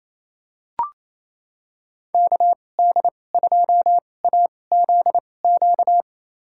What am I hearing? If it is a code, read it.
Morse code, heard as a keyed tone, about 28 wpm, spaced as for 20 wpm, 700 Hz. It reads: KD2AZQ
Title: USA Call Signs - 6 Characters - 2x Character Spacing - 28wpm